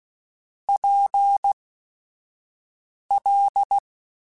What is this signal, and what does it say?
Morse code, heard as a keyed tone, about 16 wpm, spaced as for 5 wpm, 800 Hz. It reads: PL
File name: Chuckles / Chuckles-mp3s / MiddleAge @5